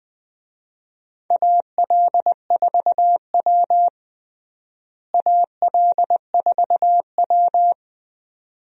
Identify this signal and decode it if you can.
Morse code, heard as a keyed tone, about 19 wpm, 700 Hz. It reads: AL4W AL4W
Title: USA Call Signs - 4 Characters - Repeated 2x - 20wpm